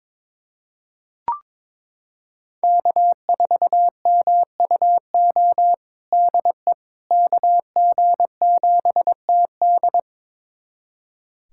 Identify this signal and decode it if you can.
Morse code, heard as a keyed tone, about 22 wpm, 700 Hz. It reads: K4MUO DE KG7TD
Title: USA Callsign DE Callsign - 5 Characters - 22wpm